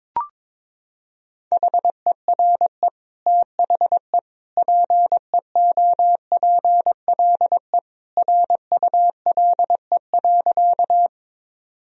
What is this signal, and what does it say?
Morse code, heard as a keyed tone, about 22 wpm, 700 Hz. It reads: HERE THE PEOPLE RULE.
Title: Sentences from Top 1000 Words - Review Part 2 - 22wpm